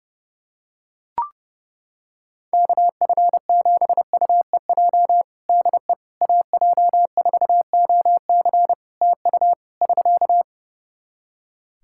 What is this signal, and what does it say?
Morse code, heard as a keyed tone, about 30 wpm, 700 Hz. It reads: KF7UEJ DE AJ4OC TU <SK>
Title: Practice QSO - Exchange 4 - Ending - 30wpm